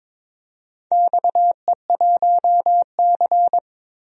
Morse code, heard as a keyed tone, about 22 wpm, 700 Hz. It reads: XE1C
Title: DXCC-100 Call Signs - 4 Characters - 22wpm